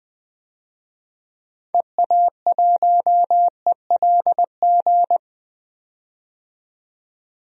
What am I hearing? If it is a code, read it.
Morse code, heard as a keyed tone, about 20 wpm, 700 Hz. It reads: EA1ELG
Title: DXCC-100 Call Signs - 6 Characters - 20wpm